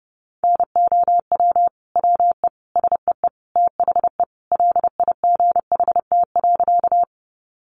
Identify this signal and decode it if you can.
Morse code, heard as a keyed tone, about 30 wpm, 700 Hz. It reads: NOW WE SEE THE LIGHT.
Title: Sentences from Top 200 Words 30wpm